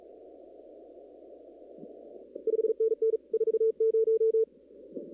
Morse code, heard as a keyed tone, about 33 wpm, 440 Hz. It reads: 5NN 40
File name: CW / kt5j wpx cw-op-k5pi